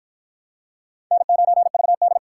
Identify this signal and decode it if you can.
Morse code, heard as a keyed tone, about 49 wpm, 700 Hz. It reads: N9VD